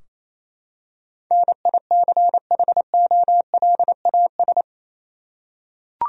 Morse code, heard as a keyed tone, about 28 wpm, 700 Hz. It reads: NICHOLAS
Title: USA Common Names - 28wpm